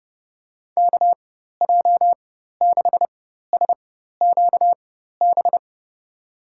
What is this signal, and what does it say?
Morse code, heard as a keyed tone, about 30 wpm, spaced as for 14 wpm, 700 Hz. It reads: KJ6SQB
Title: USA Call Signs - 6 Characters - 4x Character Spacing - 30wpm